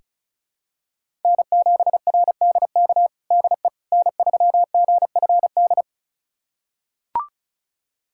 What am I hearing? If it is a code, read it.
Morse code, heard as a keyed tone, about 36 wpm, 700 Hz. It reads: N7RDK DE N3GFD
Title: USA Callsign DE Callsign - 5 Characters - 35wpm